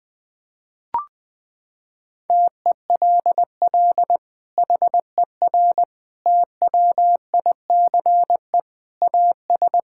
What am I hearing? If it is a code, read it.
Morse code, heard as a keyed tone, about 21 wpm, 700 Hz. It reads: TELL HER TWICE AS
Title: Sentences from Top 1600 Words 20wpm